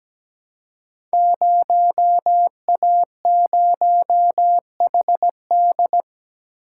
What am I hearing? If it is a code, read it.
Morse code, heard as a keyed tone, about 17 wpm, 700 Hz. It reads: 0A0HD
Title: Code Groups - 5 Letter:Numbers - 17wpm